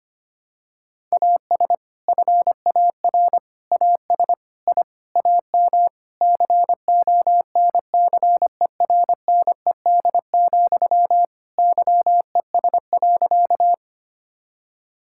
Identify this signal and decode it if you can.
Morse code, heard as a keyed tone, about 25 wpm, 700 Hz. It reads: AS FAR AS I AM CONCERNED, YES.
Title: Sentences from Top 1000 Words - Review Part 6 - 25wpm